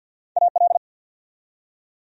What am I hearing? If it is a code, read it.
Morse code, heard as a keyed tone, about 48 wpm, 700 Hz. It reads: AL
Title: USA Common Names - Max 3 Letters - 50wpm